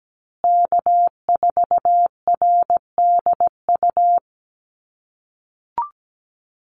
Morse code, heard as a keyed tone, about 17 wpm, 700 Hz. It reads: K4RDU
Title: USA Call Signs Repeated 3x 17wpm